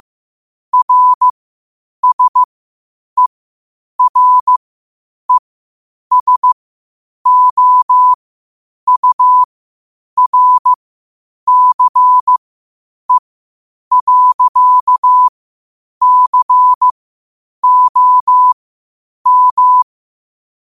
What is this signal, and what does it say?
Morse code, heard as a keyed tone, about 15 wpm, spaced as for 8 wpm, 1000 Hz. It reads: RSERESOURCE.COM
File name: qotdfiles / 8wpm